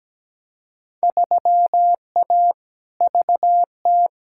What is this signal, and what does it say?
Morse code, heard as a keyed tone, about 17 wpm, 700 Hz. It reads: 3A VT
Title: Field Day Exchanges Encore - 17wpm